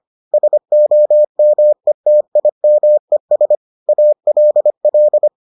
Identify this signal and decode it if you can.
Morse code, heard as a keyed tone, about 25 wpm, 600 Hz. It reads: SOMETIMES ALL